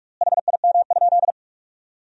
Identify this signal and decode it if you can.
Morse code, heard as a keyed tone, about 47 wpm, 700 Hz. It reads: HIM?
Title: 3x Word-Spacing - Sentences from Top 300 Words - 45wpm